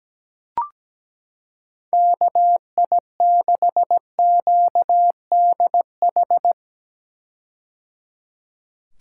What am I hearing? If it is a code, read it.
Morse code, heard as a keyed tone, about 17 wpm, 700 Hz. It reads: KI6QDH
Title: K1USN SST QSO Practice - 17wpm